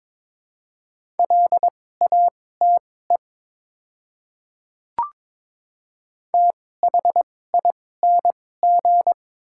Morse code, heard as a keyed tone, about 22 wpm, spaced as for 16 wpm, 700 Hz. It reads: LATE THING
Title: 2x Character Spacing - Top 300 Words - 22wpm